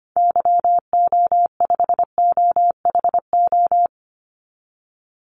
Morse code, heard as a keyed tone, about 25 wpm, 700 Hz. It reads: YO5OHO
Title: Call Signs Repeated 3x - 25wpm